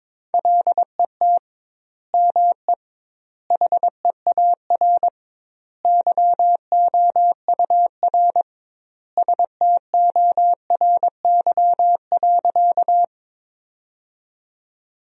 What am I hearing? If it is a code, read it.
Morse code, heard as a keyed tone, about 22 wpm, 700 Hz. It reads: LET ME HEAR YOUR STORY.
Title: 2x Word-Spacing - Sentences from Top 300 Words - 22wpm